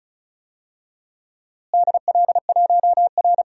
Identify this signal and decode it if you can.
Morse code, heard as a keyed tone, about 34 wpm, 700 Hz. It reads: DL1R
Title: DXCC-100 Call Signs - 4 Characters - Repeated 3x - 35wpm